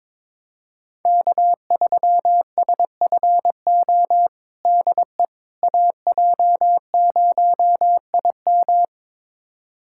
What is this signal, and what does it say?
Morse code, heard as a keyed tone, about 22 wpm, 700 Hz. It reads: K3SFO DE AJ0IM
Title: USA Callsign DE Callsign - 5 Characters - 22wpm